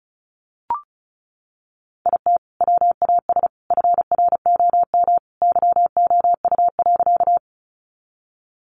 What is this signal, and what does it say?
Morse code, heard as a keyed tone, about 34 wpm, 700 Hz. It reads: IT WAS FROM YOU.
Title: Sentences from Top 100 Words 35wpm